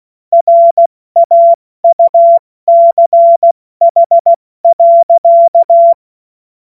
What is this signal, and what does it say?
Morse code, heard as a keyed tone, about 16 wpm, spaced as for 14 wpm, 680 Hz. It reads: RAUCH.
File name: Morsekurs / Witze / Witz 14. Juni 2022 70 F80